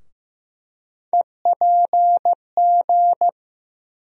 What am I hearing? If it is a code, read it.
Morse code, heard as a keyed tone, about 15 wpm, 700 Hz. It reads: EPG
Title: USA Call Sign Postfixes - 3 Letters 15wpm